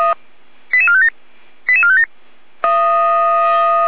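Five-tone selective call (ZVEI), heard as quick runs of five tones, tones of about 70 ms. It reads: 88457 88457
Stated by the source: ZVEI five-tone selective call received by SDR.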